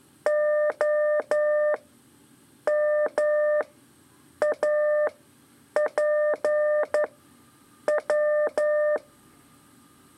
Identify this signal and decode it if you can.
Morse code, heard as a keyed tone, about 12 wpm, spaced as for 7 wpm, 600 Hz. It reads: OMAPW
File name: 3 lezione CW 2 AIR